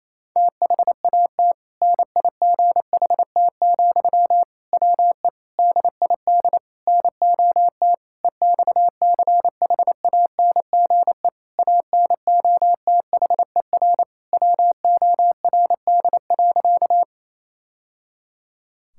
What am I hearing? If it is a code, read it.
Morse code, heard as a keyed tone, about 28 wpm, 700 Hz. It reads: THAT NIGHT, WE DID NOT EXCHANGE ANOTHER WORD.
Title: Sentences from Top 1000 Words - Review Part 6 - 28wpm